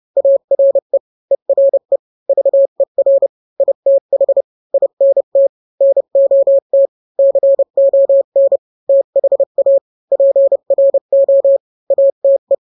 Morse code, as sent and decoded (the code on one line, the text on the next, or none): .- .-. / . .-. / ...- . .-. / .. - .... / .. -. - / -. --- - / -.-. --- -. / - .... .- / .--. .-. --- / .- - .
ARE ERE VER ITH INT NOT CON THA PRO ATE